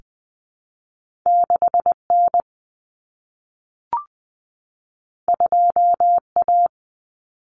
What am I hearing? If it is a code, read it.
Morse code, heard as a keyed tone, about 20 wpm, 700 Hz. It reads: 6N 2A
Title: Common Cut Numbers - 2-Digit 20wpm